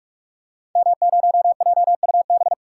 Morse code, heard as a keyed tone, about 43 wpm, 700 Hz. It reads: M0JUB
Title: Code Groups - 5 Letter:Numbers - 45wpm